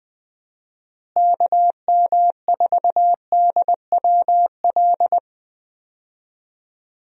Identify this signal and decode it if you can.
Morse code, heard as a keyed tone, about 21 wpm, 700 Hz. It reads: KM4DWL
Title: USA Call Signs - 6 Characters - Repeated 2x - 20wpm